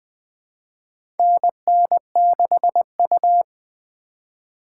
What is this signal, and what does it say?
Morse code, heard as a keyed tone, about 21 wpm, 700 Hz. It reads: NN6U
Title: USA Call Signs - 2x Character Spacing - 20wpm